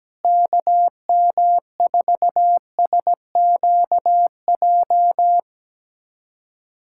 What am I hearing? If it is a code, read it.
Morse code, heard as a keyed tone, about 17 wpm, 700 Hz. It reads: KM4SQJ
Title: USA Callsign DE Callsign - 6 Characters - 17wpm